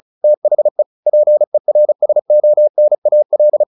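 Morse code, heard as a keyed tone, about 36 wpm, 600 Hz. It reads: THE PERSONAL
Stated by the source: Morse practice at 35 wpm from CW med Gnister, sending Ken Follett's novel Never in English.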